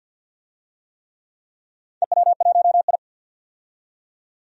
Morse code, heard as a keyed tone, about 49 wpm, 700 Hz. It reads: EW1I
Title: DXCC-100 Call Signs - 4 Characters - 50wpm